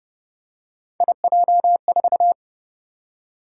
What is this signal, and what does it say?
Morse code, heard as a keyed tone, about 29 wpm, 700 Hz. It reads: IJ4
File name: Speed-Racing - Three Letter-Number Combinations - 25wpm